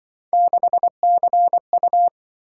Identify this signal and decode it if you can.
Morse code, heard as a keyed tone, about 25 wpm, 700 Hz. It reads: 6CU
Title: Speed-Racing - Three Letter-Number Combinations - 20wpm